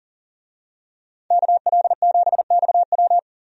Morse code, heard as a keyed tone, about 40 wpm, 700 Hz. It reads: KL7XW
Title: DXCC-100 Call Signs - 5 Characters - Repeated 2x - 40wpm